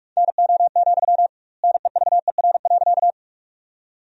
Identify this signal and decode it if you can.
Morse code, heard as a keyed tone, about 47 wpm, 700 Hz. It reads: NO, NEVER.
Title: 2x Word-Spacing - Sentences from Top 300 Words - 45wpm